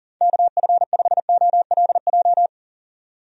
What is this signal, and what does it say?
Morse code, heard as a keyed tone, about 40 wpm, 700 Hz. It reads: KF5OLJ
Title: USA Call Signs - 6 Characters - 40wpm